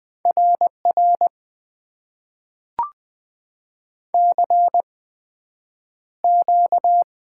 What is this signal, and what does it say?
Morse code, heard as a keyed tone, about 20 wpm, spaced as for 9 wpm, 700 Hz. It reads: RR CQ
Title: Top 90 QSO Elements - 8x Character Spacing - 20wpm